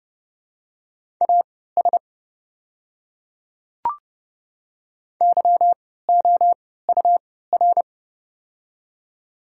Morse code, heard as a keyed tone, about 30 wpm, spaced as for 17 wpm, 700 Hz. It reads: AS YOUR
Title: 3x Character Spacing - Top 100 Words - 30wpm